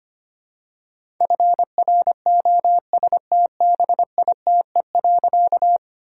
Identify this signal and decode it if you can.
Morse code, heard as a keyed tone, about 25 wpm, 700 Hz. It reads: FROSTBITE.
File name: practice / 25wpm / All-Age Appropriate Jokes - 25wpm